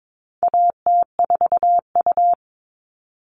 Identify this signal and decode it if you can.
Morse code, heard as a keyed tone, about 22 wpm, 700 Hz. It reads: AT4U
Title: All Cut Numbers - 4-Digit 22wpm